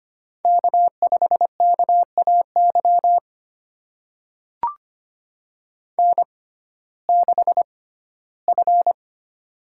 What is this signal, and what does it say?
Morse code, heard as a keyed tone, about 25 wpm, spaced as for 15 wpm, 700 Hz. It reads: K5KAY N6F
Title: USA Call Signs - 5 Characters - 6x Character Spacing - 25wpm